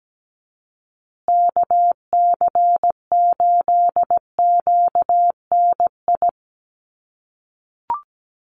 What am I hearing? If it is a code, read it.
Morse code, heard as a keyed tone, about 17 wpm, 700 Hz. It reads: KC8QNI
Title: USA Call Signs Repeated 2x 17wpm